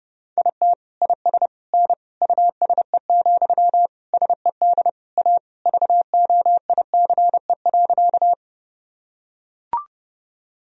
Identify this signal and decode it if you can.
Morse code, heard as a keyed tone, about 30 wpm, 700 Hz. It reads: IT IS N USE, SED A VOICE.
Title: Sentences from Top 500 Words and Abbreviations 30wpm